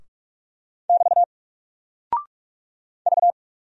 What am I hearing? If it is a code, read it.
Morse code, heard as a keyed tone, about 44 wpm, 700 Hz. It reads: <BT> V
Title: Single Character (Letter:Number:Punctuation:Prosign) 45wpm